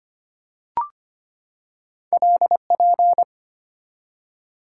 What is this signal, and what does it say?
Morse code, heard as a keyed tone, about 25 wpm, 690 Hz. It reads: LP
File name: Two Letter-Number Combinations - 25wpm